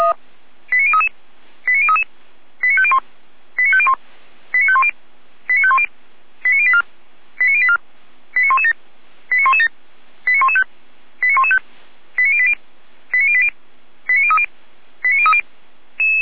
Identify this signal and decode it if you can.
Five-tone selective call (ZVEI), heard as quick runs of five tones, tones of about 70 ms. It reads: 89033 89033 89501 89501 89510 89510 89084 89084 89117 89117 89115 89115 89080 89080 89030 89030
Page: SDR/ZVEI